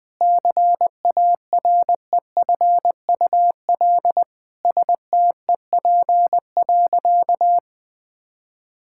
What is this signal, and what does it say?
Morse code, heard as a keyed tone, about 20 wpm, 700 Hz. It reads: CAREFUL STEP.
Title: Sentences from Top 1600 Words 20wpm